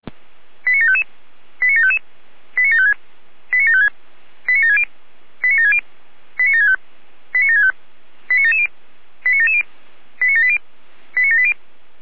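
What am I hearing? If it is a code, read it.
Five-tone selective call (ZVEI), heard as quick runs of five tones, tones of about 70 ms. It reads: 89755 89755 89756 89756 89760 89760 89765 89765 89770 89770 89780 89780
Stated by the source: ZVEI five-tone selective call received on SDR.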